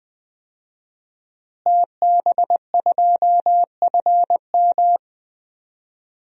Morse code, heard as a keyed tone, about 20 wpm, 700 Hz. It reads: TB2FM
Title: DXCC-100 Call Signs - 5 Characters - Repeated 2x - 20wpm